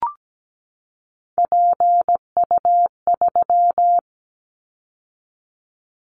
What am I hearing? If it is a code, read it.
Morse code, heard as a keyed tone, about 17 wpm, 700 Hz. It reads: PU3
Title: Callsign Prefixes for Top 100 DXCC - 17wpm